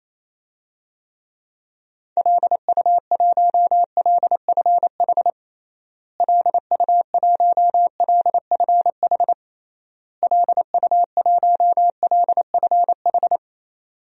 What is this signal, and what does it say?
Morse code, heard as a keyed tone, about 28 wpm, 700 Hz. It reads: LU1LFH LU1LFH LU1LFH
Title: DXCC-100 Call Signs - 6 Characters - Repeated 3x - 28wpm